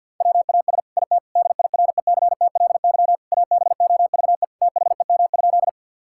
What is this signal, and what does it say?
Morse code, heard as a keyed tone, about 50 wpm, 700 Hz. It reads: WAS IT DIRECTLY ABOVE THEM?
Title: Sentences from Top 1000 Words - Review Part 5 - 50wpm